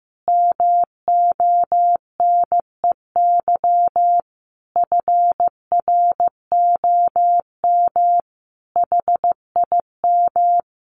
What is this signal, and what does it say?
Morse code, heard as a keyed tone, about 15 wpm, 700 Hz. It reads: MONEY FROM HIM
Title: Sentences from Top 1300 Words 15wpm